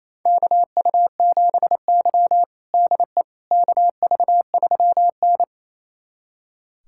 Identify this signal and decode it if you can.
Morse code, heard as a keyed tone, about 28 wpm, 700 Hz. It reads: KU7Y DE KV3N
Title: USA Callsign DE Callsign - 4 Characters - 28wpm